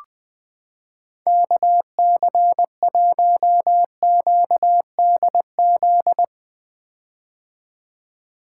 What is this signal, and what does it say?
Morse code, heard as a keyed tone, about 20 wpm, 700 Hz. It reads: KC1QDZ
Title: 2023 POTA Hunter Call Signs - 20wpm